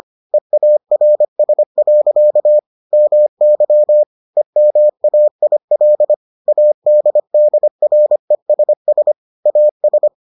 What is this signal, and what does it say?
Morse code, heard as a keyed tone, about 25 wpm, 600 Hz. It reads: EARS. MY EMAIL ADDRESS AS